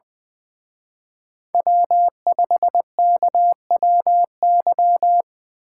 Morse code, heard as a keyed tone, about 24 wpm, 700 Hz. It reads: W5KWY
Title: Speed Racing - USA Call Signs 20wpm